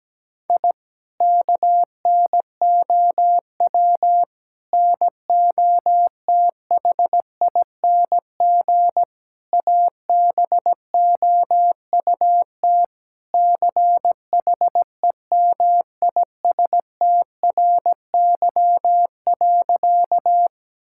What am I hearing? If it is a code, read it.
Morse code, heard as a keyed tone, about 17 wpm, 700 Hz. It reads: I KNOW NOTHING ABOUT CHEMISTRY.
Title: Sentences from Top 1000 Words - Review Part 6 - 17wpm